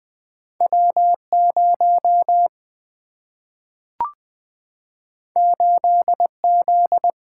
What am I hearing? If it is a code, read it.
Morse code, heard as a keyed tone, about 20 wpm, 700 Hz. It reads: W0 8Z
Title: Single-Letter Single-Number 20wpm